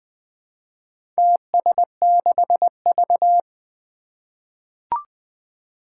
Morse code, heard as a keyed tone, about 20 wpm, 700 Hz. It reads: TS6V
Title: Four Challenging Letter-Number Combinations 20wpm